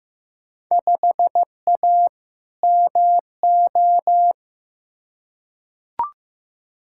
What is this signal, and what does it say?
Morse code, heard as a keyed tone, about 15 wpm, 700 Hz. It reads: 5A MO
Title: Field Day Exchanges Encore - 15wpm